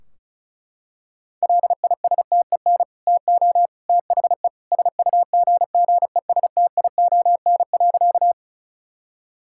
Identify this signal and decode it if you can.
Morse code, heard as a keyed tone, about 34 wpm, 700 Hz. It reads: LISTEN TO THE SUGGESTION.
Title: Sentences from Top 1600 Words 35wpm